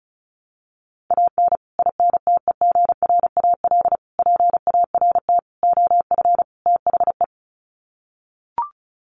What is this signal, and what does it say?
Morse code, heard as a keyed tone, about 34 wpm, 700 Hz. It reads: AN INTEGRAL PART OF THE
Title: Sets of 5 Words 35wpm